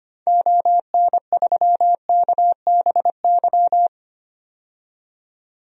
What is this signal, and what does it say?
Morse code, heard as a keyed tone, about 25 wpm, 700 Hz. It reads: ON3KBY
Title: DXCC-100 Call Signs - 6 Characters - 25wpm